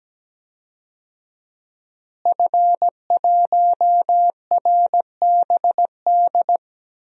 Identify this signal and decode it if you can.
Morse code, heard as a keyed tone, about 17 wpm, 700 Hz. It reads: F1RBD
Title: DXCC-100 Call Signs - 5 Characters - Repeated 2x - 17wpm